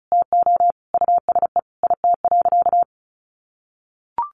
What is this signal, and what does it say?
Morse code, heard as a keyed tone, about 34 wpm, 700 Hz. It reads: TO USE IT.
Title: Sentences from Top 500 Words 35wpm